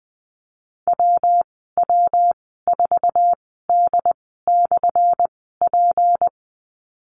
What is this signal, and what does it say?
Morse code, heard as a keyed tone, about 20 wpm, spaced as for 14 wpm, 700 Hz. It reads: WW4D/P
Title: 179 Lesson 29g - Slash - Call Signs 20wpm